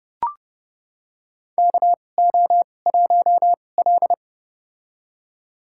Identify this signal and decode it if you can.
Morse code, heard as a keyed tone, about 30 wpm, spaced as for 22 wpm, 700 Hz. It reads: KO1L
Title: USA Call Signs - 4 Characters - 2x Character Spacing - 30wpm